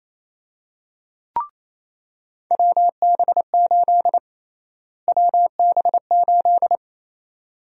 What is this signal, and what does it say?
Morse code, heard as a keyed tone, about 27 wpm, 700 Hz. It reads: WB8 WB8
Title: USA Call Sign Prefixes - Repeated 2x - 28wpm